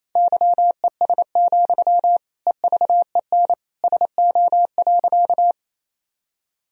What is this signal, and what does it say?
Morse code, heard as a keyed tone, about 28 wpm, 700 Hz. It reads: YES, EVEN SO.
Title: Sentences from Top 1000 Words - Review Part 1 - 28wpm